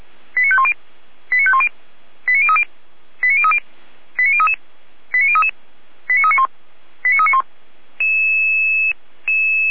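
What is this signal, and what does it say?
Five-tone selective call (ZVEI), heard as quick runs of five tones, tones of about 70 ms. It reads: 89510 89510 89030 89030 89033 89033 89391 89391
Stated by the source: ZVEI five-tone call 89050 received via SDR.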